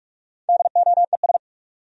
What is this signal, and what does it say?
Morse code, heard as a keyed tone, about 58 wpm, 700 Hz. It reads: DOES